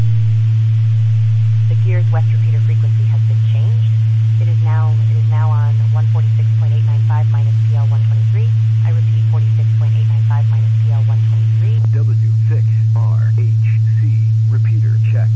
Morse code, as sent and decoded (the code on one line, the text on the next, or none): none